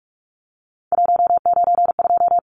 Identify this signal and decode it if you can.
Morse code, heard as a keyed tone, about 44 wpm, 700 Hz. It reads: 192
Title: Numbers - 3-Digit 45wpm